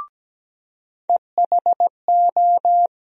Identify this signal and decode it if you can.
Morse code, heard as a keyed tone, about 17 wpm, 700 Hz. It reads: EHO
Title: USA Call Sign Postfixes - 3 Letters 17wpm